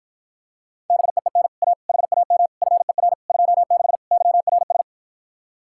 Morse code, heard as a keyed tone, about 54 wpm, 700 Hz. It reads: BEEN A HAM FER 26 YRS